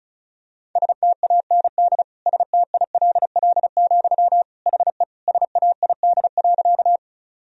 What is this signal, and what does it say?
Morse code, heard as a keyed tone, about 36 wpm, 700 Hz. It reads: STAND STILL, HE SAID.
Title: Sentences from Top 1000 Words - Review Part 1 - 35wpm